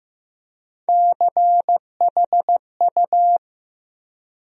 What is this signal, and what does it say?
Morse code, heard as a keyed tone, about 15 wpm, 700 Hz. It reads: CHU
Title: USA Call Sign Postfixes - 3 Letters 15wpm